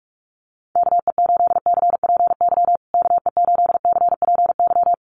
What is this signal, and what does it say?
Morse code, heard as a keyed tone, about 44 wpm, 690 Hz. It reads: KE8CPY KE8CPY
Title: USA Call Signs Repeated 3x 45wpm